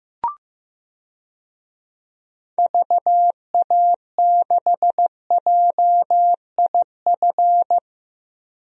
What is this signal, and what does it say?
Morse code, heard as a keyed tone, about 15 wpm, 700 Hz. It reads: VA6JIF
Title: DXCC-100 Call Signs - 6 Characters - Repeated 3x - 15wpm